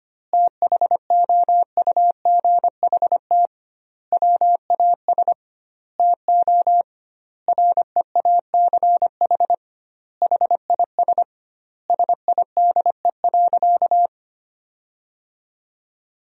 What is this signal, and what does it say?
Morse code, heard as a keyed tone, about 25 wpm, 700 Hz. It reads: THOUGHT WAS TO REACH HIS SIDE.
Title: 2x Word-Spacing - Sentences from Top 400 Words - 25wpm